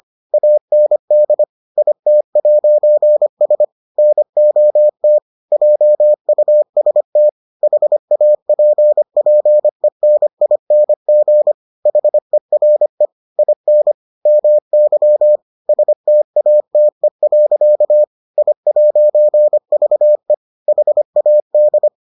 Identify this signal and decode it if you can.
Morse code, heard as a keyed tone, about 25 wpm, 600 Hz. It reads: AND IT'S NOT JUST HAPPENING HERE IN MY STATE. I'VE HAD